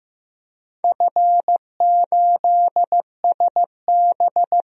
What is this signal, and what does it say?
Morse code, heard as a keyed tone, about 15 wpm, 700 Hz. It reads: F8SB
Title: DXCC-100 Call Signs - 4 Characters - 15wpm